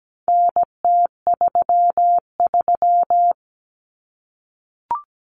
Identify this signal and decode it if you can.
Morse code, heard as a keyed tone, about 17 wpm, 700 Hz. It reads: NT33
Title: Common Cut Numbers - 4-Digit 17wpm